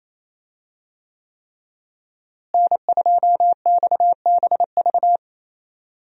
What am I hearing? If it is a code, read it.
Morse code, heard as a keyed tone, about 28 wpm, 700 Hz. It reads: N2XBV
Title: USA Call Signs - 5 Characters - Repeated 3x - 28wpm